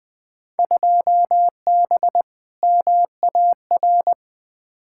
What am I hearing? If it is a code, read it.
Morse code, heard as a keyed tone, about 20 wpm, 700 Hz. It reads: 2B MAR
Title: Field Day Exchanges Encore - 20wpm